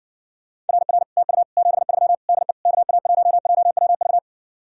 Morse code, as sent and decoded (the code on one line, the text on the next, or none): .-. .-. / - ..- / --... ...-- / -.. / -.- .- .---- .--- .-- ...-
RR TU 73 DE KA1JWV